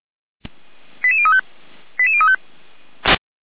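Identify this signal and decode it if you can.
Five-tone selective call (ZVEI), heard as quick runs of five tones, tones of about 70 ms. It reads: 88035 88035
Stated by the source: ZVEI five-tone selective call received on an SDR.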